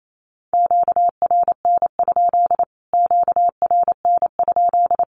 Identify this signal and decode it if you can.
Morse code, heard as a keyed tone, about 28 wpm, 700 Hz. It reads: QRN? QRN?
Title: Q-Codes Repeated 3x 28wpm